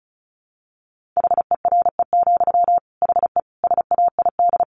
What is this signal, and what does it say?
Morse code, heard as a keyed tone, about 34 wpm, 700 Hz. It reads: HERE, HE SAID
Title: Sentences from Top 1500 Words 35wpm